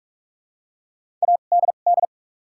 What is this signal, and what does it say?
Morse code, heard as a keyed tone, about 44 wpm, spaced as for 32 wpm, 700 Hz. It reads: ADD
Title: 2x Character Spacing - Top 300 Words - 45wpm